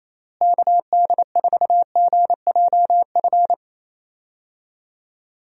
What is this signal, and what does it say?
Morse code, heard as a keyed tone, about 28 wpm, 700 Hz. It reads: KD4GJF
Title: USA Call Signs - 6 Characters - Repeated 3x - 28wpm